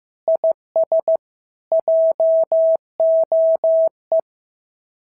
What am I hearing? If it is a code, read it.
Morse code, heard as a keyed tone, about 14 wpm, 650 Hz. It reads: IS JOE
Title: SOTA QSO Practice 15wpm